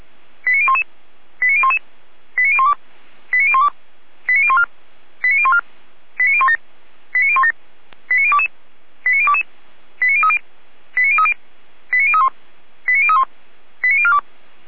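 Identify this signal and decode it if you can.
Five-tone selective call (ZVEI), heard as quick runs of five tones, tones of about 80 ms. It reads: 89011 89011 89012 89012 89014 89014 89017 89017 89022 89022 89030 89030 89031 89031 89042
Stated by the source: ZVEI five-tone selective call 89010, received by SDR, from the Feuerwehr Orlamünde fire brigade.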